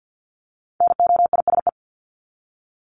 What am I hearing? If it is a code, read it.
Morse code, heard as a keyed tone, about 50 wpm, 700 Hz. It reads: NOISE
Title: Top 1501-1750 Words 50wpm